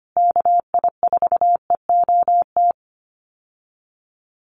Morse code, heard as a keyed tone, about 25 wpm, 700 Hz. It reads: KI4EOT
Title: USA Call Signs Repeated 3x 25wpm